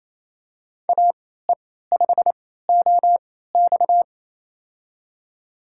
Morse code, heard as a keyed tone, about 27 wpm, spaced as for 16 wpm, 690 Hz. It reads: AE5OX
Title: USA Call Signs - 3x Character Spacing - 28wpm